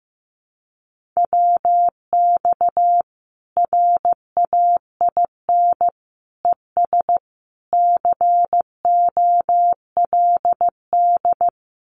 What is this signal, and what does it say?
Morse code, heard as a keyed tone, about 15 wpm, 700 Hz. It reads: WX RAIN ES COLD
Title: Common QSO Phrases 15wpm